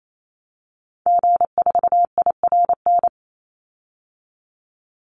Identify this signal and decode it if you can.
Morse code, heard as a keyed tone, about 28 wpm, 700 Hz. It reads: G4IRN
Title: Call Signs - 28wpm